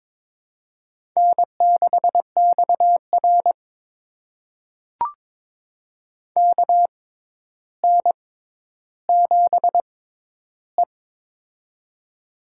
Morse code, heard as a keyed tone, about 22 wpm, spaced as for 11 wpm, 700 Hz. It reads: N6XR KN7E
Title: USA Call Signs - 4 Characters - 6x Character Spacing - 22wpm